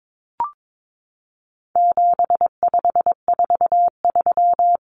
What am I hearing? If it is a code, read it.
Morse code, heard as a keyed tone, about 22 wpm, 700 Hz. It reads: 7543
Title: Common Cut Numbers - 4-Digit 22wpm